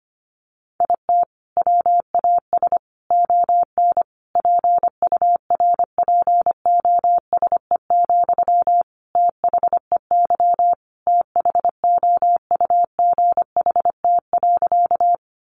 Morse code, heard as a keyed tone, about 25 wpm, 700 Hz. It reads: IT WAS ON PURPOSE, THEY THOUGHT.
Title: Sentences from Top 800 Words 25wpm